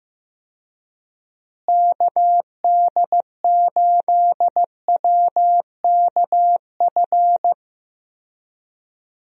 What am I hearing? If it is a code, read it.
Morse code, heard as a keyed tone, about 15 wpm, 700 Hz. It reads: KD8WKF
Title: USA Call Signs - 6 Characters - Repeated 3x - 15wpm